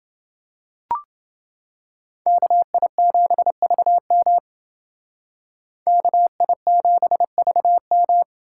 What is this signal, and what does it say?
Morse code, heard as a keyed tone, about 29 wpm, 700 Hz. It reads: KI7VM KI7VM